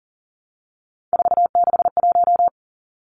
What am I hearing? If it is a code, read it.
Morse code, heard as a keyed tone, about 40 wpm, 700 Hz. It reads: 461
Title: Numbers - 3-Digit 40wpm